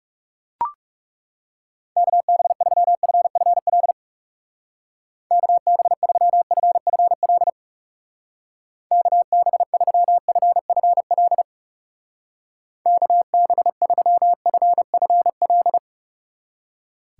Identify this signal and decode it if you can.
Morse code, heard as a keyed tone, about 37 wpm, 700 Hz. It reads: KB3FFL KB3FFL KB3FFL KB3FFL
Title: Speed Racing - USA 2x3 Call Signs 30wpm